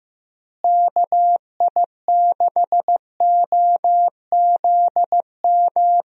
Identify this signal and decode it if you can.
Morse code, heard as a keyed tone, about 15 wpm, 700 Hz. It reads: KI6OZM
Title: USA Call Signs - 6 Characters - Repeated 3x - 15wpm